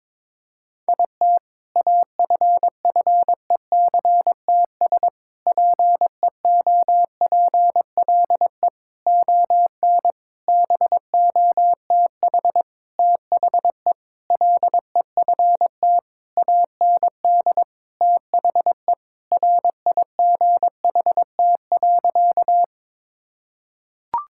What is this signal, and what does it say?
Morse code, heard as a keyed tone, about 22 wpm, 700 Hz. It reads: IT AFFECTS PEOPLE ON BOTH THE LEFT AND THE RIGHT.
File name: Sentences with 7-Word Phrases - 22wpm